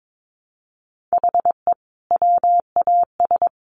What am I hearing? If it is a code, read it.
Morse code, heard as a keyed tone, about 22 wpm, 700 Hz. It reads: HE WAS
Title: Sentences from Top 1500 Words 22wpm